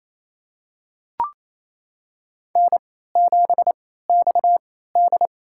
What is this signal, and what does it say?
Morse code, heard as a keyed tone, about 28 wpm, spaced as for 16 wpm, 700 Hz. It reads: N7XD
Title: USA Call Signs - 4 Characters - 3x Character Spacing - 28wpm